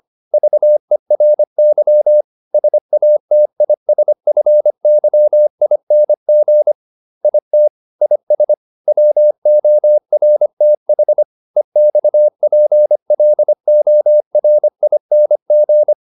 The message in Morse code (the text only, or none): VERY SATISFYING IT IS WORTH EXPLORING